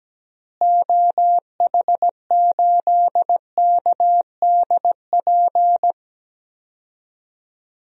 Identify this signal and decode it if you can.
Morse code, heard as a keyed tone, about 17 wpm, 700 Hz. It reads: OH8KDP
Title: DXCC-100 Call Signs - 6 Characters - Repeated 2x - 17wpm